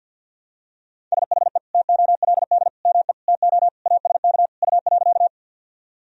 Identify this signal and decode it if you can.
Morse code, heard as a keyed tone, about 47 wpm, 700 Hz. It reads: SHE TOLD ME TO ASK U.
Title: Sentences from Top 400 Words and Abbreviations 50wpm